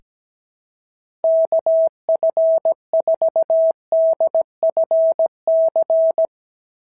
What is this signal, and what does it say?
Morse code, heard as a keyed tone, about 17 wpm, 650 Hz. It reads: KF4DFC
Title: SOTA QSO Practice 17wpm